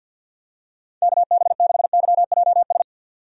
Morse code, heard as a keyed tone, about 53 wpm, 700 Hz. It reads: KB6YJS
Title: USA Call Signs - 6 Characters - 50wpm